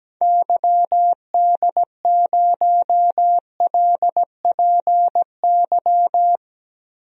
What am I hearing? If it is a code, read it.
Morse code, heard as a keyed tone, about 17 wpm, 700 Hz. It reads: YD0LPY
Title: DXCC-100 Call Signs - 6 Characters - 17wpm